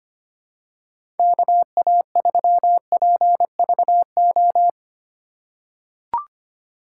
Morse code, heard as a keyed tone, about 25 wpm, 700 Hz. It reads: KA3PVO
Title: USA Call Signs - 6 Characters - 8x Character Spacing - 25wpm